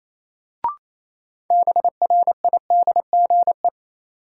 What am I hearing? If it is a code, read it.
Morse code, heard as a keyed tone, about 28 wpm, 700 Hz. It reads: BRIDGE
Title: Sentences from Top 1200 Words - New Word Drill Alt - 28wpm